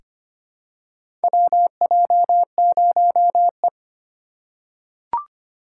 Morse code, heard as a keyed tone, about 25 wpm, 700 Hz. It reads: WJ0E
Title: USA Call Signs - 4 Characters - 3x Character Spacing - 25wpm